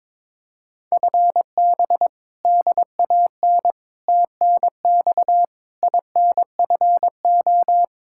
Morse code, heard as a keyed tone, about 22 wpm, 700 Hz. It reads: FB DAN TNX INFO
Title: Practice QSO - Exchange 3 - More Info - 22wpm